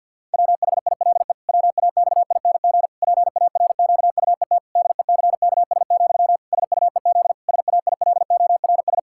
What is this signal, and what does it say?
Morse code, heard as a keyed tone, about 52 wpm, 700 Hz. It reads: WHILE WAXING PARQUET DECKS, SUEZ SAILORS
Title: Pangram Challenge v2 - 50wpm